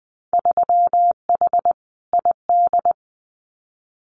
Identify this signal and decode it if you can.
Morse code, heard as a keyed tone, about 20 wpm, 700 Hz. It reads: 3H ID
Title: Winter Field Day Exchanges Encore 20wpm